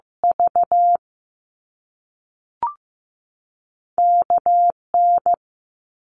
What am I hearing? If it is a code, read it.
Morse code, heard as a keyed tone, about 15 wpm, 700 Hz. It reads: V KN
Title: Top Two Letter Combinations in Words 15wpm